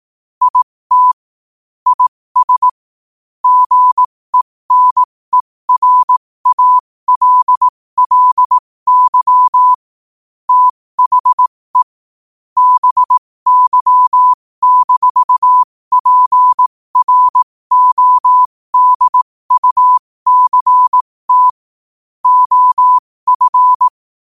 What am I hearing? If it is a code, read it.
Morse code, heard as a keyed tone, about 18 wpm, spaced as for 15 wpm, 1000 Hz. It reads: IT IS GENERALLY THE BY-PRODUCT OF
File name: qotdfiles / 15wpm